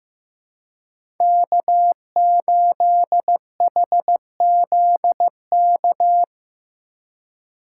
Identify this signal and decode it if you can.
Morse code, heard as a keyed tone, about 15 wpm, 700 Hz. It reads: K8HZK
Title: USA Call Signs - 2x Character Spacing - 15wpm